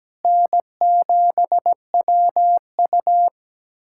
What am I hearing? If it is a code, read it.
Morse code, heard as a keyed tone, about 17 wpm, 700 Hz. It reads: N7WU
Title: USA Callsign DE Callsign - 4 Characters - 17wpm